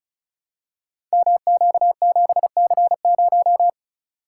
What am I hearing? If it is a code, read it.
Morse code, heard as a keyed tone, about 34 wpm, 690 Hz. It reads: MQ7C0
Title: Code Groups - 5 Letter:Numbers - 35wpm